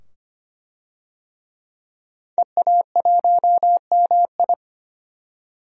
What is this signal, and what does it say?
Morse code, heard as a keyed tone, about 25 wpm, 700 Hz. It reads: EA1MI
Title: DXCC-100 Call Signs - 5 Characters - 25wpm